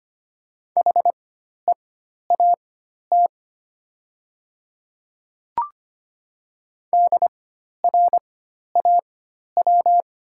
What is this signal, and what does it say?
Morse code, heard as a keyed tone, about 25 wpm, spaced as for 12 wpm, 700 Hz. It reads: HEAT DRAW
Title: 4x Character Spacing - Top 500 Words - 25wpm